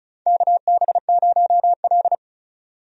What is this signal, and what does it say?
Morse code, heard as a keyed tone, about 34 wpm, 700 Hz. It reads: KB0L
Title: USA Call Signs - 4 Characters - 35wpm